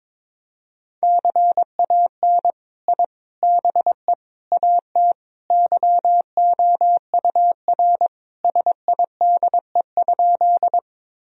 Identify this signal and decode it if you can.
Morse code, heard as a keyed tone, about 22 wpm, 700 Hz. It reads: CAN I BE AT YOUR SIDE?
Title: Sentences from Top 1000 Words - Review Part 1 - 22wpm